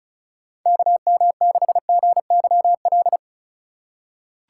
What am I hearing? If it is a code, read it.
Morse code, heard as a keyed tone, about 36 wpm, 700 Hz. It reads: KM6GYL
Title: USA Call Signs - 2x Character Spacing - 35wpm